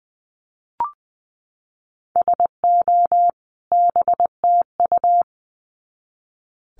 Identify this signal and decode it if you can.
Morse code, heard as a keyed tone, about 20 wpm, 700 Hz. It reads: SO BTU
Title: Common QSO Phrases 20wpm